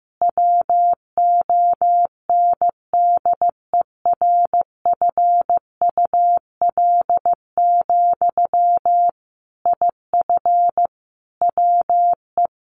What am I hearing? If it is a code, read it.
Morse code, heard as a keyed tone, about 15 wpm, 700 Hz. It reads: WONDERFUL, IF WE
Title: Sentences from Top 1100 Words 15wpm